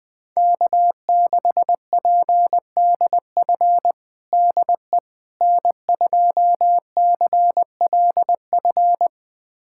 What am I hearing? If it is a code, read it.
Morse code, heard as a keyed tone, about 20 wpm, 700 Hz. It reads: K6PDF DE N2CLF
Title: USA Callsign DE Callsign - 5 Characters - 20wpm